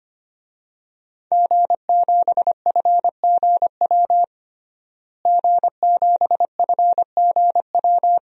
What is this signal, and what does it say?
Morse code, heard as a keyed tone, about 25 wpm, 700 Hz. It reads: G7FGW G7FGW
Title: DXCC-100 Call Signs - 5 Characters - Repeated 2x - 25wpm